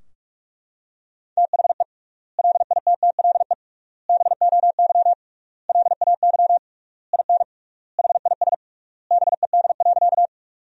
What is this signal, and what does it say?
Morse code, heard as a keyed tone, about 42 wpm, 700 Hz. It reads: THE LITTLE BOY LAY IN HIS BED.
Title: 3x Word-Spacing - Sentences from Top 500 Words - 45wpm